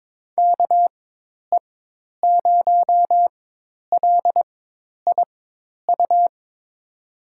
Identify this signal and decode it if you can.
Morse code, heard as a keyed tone, about 21 wpm, spaced as for 10 wpm, 700 Hz. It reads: KE0LIU
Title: USA Call Signs - 6 Characters - 4x Character Spacing - 22wpm